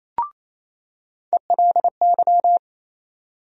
Morse code, heard as a keyed tone, about 28 wpm, 700 Hz. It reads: ELY
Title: USA Call Sign Postfixes - 3 Letters 28wpm